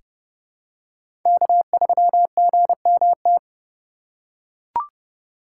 Morse code, heard as a keyed tone, about 31 wpm, 700 Hz. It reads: K3GMT
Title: K1USN SST QSO Practice - 30wpm